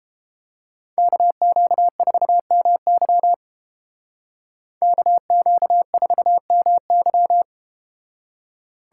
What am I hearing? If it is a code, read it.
Morse code, heard as a keyed tone, about 30 wpm, 700 Hz. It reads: KQ4MY KQ4MY